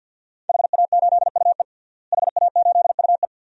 Morse code, heard as a keyed tone, about 53 wpm, 700 Hz. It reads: HA8UE HA8UE
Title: DXCC-100 Call Signs - 5 Characters - Repeated 2x - 50wpm